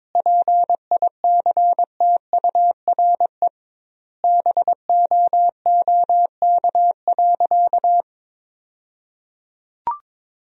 Morse code, heard as a keyed tone, about 22 wpm, 700 Hz. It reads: PICTURE BOOK.
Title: 2x Word-Spacing - Sentences from Top 500 Words - 22wpm